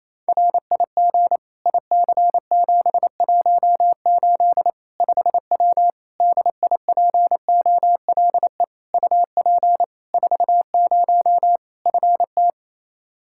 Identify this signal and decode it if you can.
Morse code, heard as a keyed tone, about 28 wpm, 700 Hz. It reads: RIG IC718 5W DIPOLE UP 40 FT
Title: K1USN SST QSO Practice 28wpm